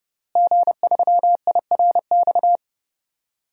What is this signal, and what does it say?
Morse code, heard as a keyed tone, about 30 wpm, 700 Hz. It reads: G3IRX
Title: DXCC-100 Call Signs - 5 Characters - Repeated 3x - 30wpm